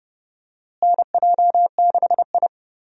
Morse code, heard as a keyed tone, about 30 wpm, 700 Hz. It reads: NJ6I
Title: USA Call Signs - 4 Characters - Repeated 2x - 30wpm